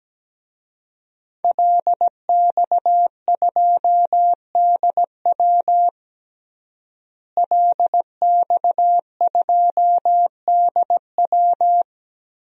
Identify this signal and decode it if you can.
Morse code, heard as a keyed tone, about 17 wpm, 700 Hz. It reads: LX2DW LX2DW
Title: DXCC-100 Call Signs - 5 Characters - Repeated 2x - 17wpm